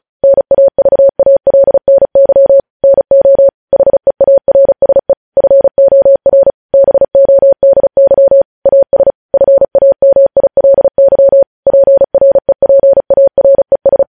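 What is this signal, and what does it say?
Morse code, heard as a keyed tone, about 36 wpm, 550 Hz. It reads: NAVALNY NO HEARSE FOR BODY AS FAMILY PREPARES
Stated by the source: News headlines sent in Morse at speed 35.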